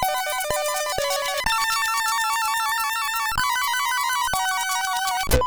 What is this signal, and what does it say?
There is no code.